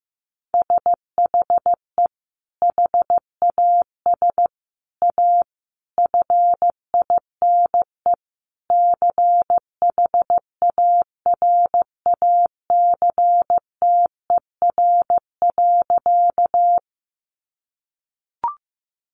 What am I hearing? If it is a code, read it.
Morse code, heard as a keyed tone, about 15 wpm, 700 Hz. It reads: SHE HAS A FINE CHARACTER.
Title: Sentences from Top 1200 Words 15wpm